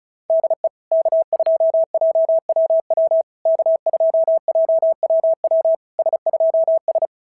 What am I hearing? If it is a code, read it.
Morse code, heard as a keyed tone, about 33 wpm, 650 Hz. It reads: DE K2JWW K2JWW S2S
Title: SOTA QSO Practice 35wpm